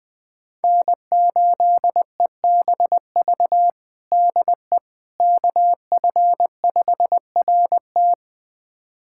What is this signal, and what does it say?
Morse code, heard as a keyed tone, about 20 wpm, 700 Hz. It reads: N8EBV DE KF5RT
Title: USA Callsign DE Callsign - 5 Characters - 20wpm